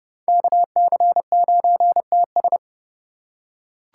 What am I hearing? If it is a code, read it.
Morse code, heard as a keyed tone, about 29 wpm, 700 Hz. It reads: KC9TS